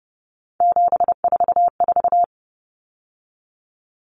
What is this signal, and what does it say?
Morse code, heard as a keyed tone, about 30 wpm, 700 Hz. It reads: 744
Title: Common Cut Numbers - 3-Digit 30wpm